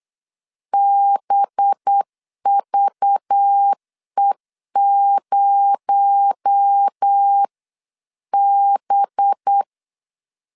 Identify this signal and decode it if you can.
Morse code, heard as a keyed tone, about 8 wpm, 800 Hz. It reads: BVE0B
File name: 8wpm